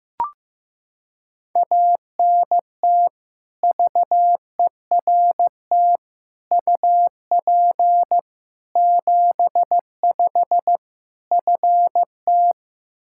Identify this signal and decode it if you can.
Morse code, heard as a keyed tone, about 15 wpm, 700 Hz. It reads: ANT VERT UP 75 FT
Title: Practice QSO - Exchange 2 - Info - 15wpm